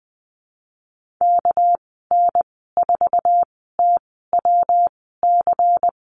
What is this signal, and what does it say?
Morse code, heard as a keyed tone, about 20 wpm, spaced as for 15 wpm, 700 Hz. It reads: KN4TWC
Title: 200 Lesson 32g - Letter k - Call Signs 20wpm